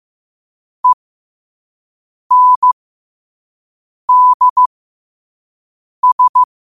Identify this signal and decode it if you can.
Morse code, heard as a keyed tone, about 15 wpm, spaced as for 5 wpm, 1000 Hz. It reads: ENDS
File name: qotdfiles / 5wpm